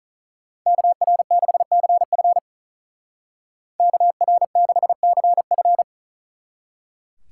Morse code, heard as a keyed tone, about 47 wpm, 700 Hz. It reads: KR6CF KR6CF